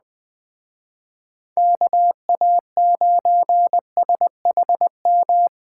Morse code, heard as a keyed tone, about 20 wpm, 700 Hz. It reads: KA9SHM